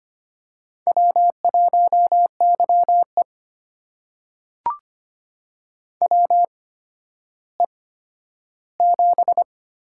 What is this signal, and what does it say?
Morse code, heard as a keyed tone, about 25 wpm, spaced as for 12 wpm, 700 Hz. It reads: W1YE WE7
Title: USA Call Signs - 4 Characters - 8x Character Spacing - 25wpm